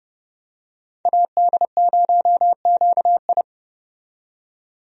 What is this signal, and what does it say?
Morse code, heard as a keyed tone, about 30 wpm, 700 Hz. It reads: AD0QI
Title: USA Call Signs - 5 Characters - 30wpm